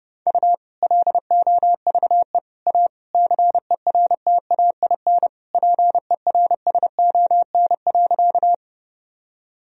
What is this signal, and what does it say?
Morse code, heard as a keyed tone, about 30 wpm, 700 Hz. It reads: U LOVE A CERTAIN PERSON.
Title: Sentences from Top 500 Words and Abbreviations 30wpm